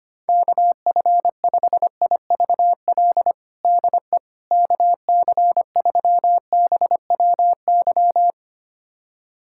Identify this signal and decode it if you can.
Morse code, heard as a keyed tone, about 25 wpm, 700 Hz. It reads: KF5IVL DE KC3BWY
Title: USA Callsign DE Callsign - 6 Characters - 25wpm